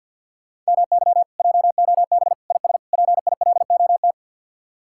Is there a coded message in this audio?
Yes: Morse code from 0.7 to 4.1 s, heard as a keyed tone, about 64 wpm, 700 Hz. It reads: MY JOB IS PILOT